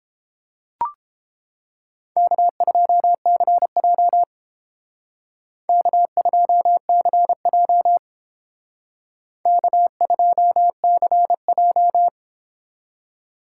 Morse code, heard as a keyed tone, about 31 wpm, 700 Hz. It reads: K2CJ K2CJ K2CJ